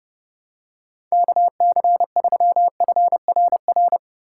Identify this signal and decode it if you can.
Morse code, heard as a keyed tone, about 32 wpm, 700 Hz. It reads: KC3FRR
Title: USA Call Signs - 6 Characters - 3x Character Spacing - 30wpm